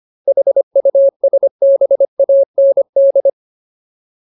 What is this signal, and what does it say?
Morse code, heard as a keyed tone, about 25 wpm, 550 Hz. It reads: HUSBAND
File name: Ultimate Word List / 25WPM Ultimate